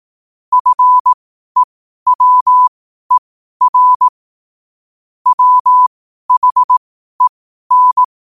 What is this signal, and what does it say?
Morse code, heard as a keyed tone, about 18 wpm, spaced as for 12 wpm, 1000 Hz. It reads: FEWER WHEN
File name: qotdfiles / 12wpm